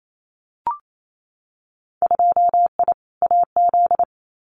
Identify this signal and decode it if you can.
Morse code, heard as a keyed tone, about 28 wpm, 700 Hz. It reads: 2I AZ
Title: Winter Field Day Exchanges 28wpm